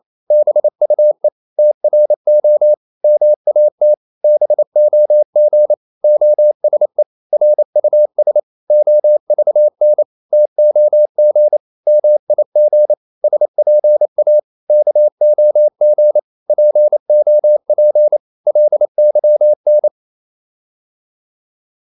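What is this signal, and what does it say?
Morse code, heard as a keyed tone, about 28 wpm, 600 Hz. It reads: BUE TRO MAT BOG OSE RUS OVN TOG MIG SPA KOG POP LYN
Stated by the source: Danish short words sent in Morse at 28 wpm, from CW med Gnister.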